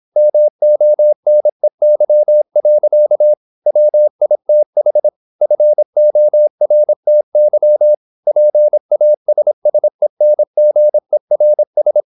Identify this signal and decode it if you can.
Morse code, heard as a keyed tone, about 26 wpm, 600 Hz. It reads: MONEY. WITH FORTY PASSENGERS
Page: Never 26wpm | CW med Gnister